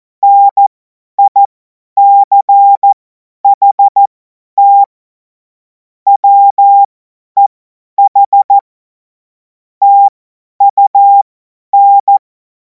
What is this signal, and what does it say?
Morse code, heard as a keyed tone, about 14 wpm, spaced as for 10 wpm, 800 Hz. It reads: NICHT WEH TUN
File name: Morsekurs / Witze / Witz 07. April 2026 50 F70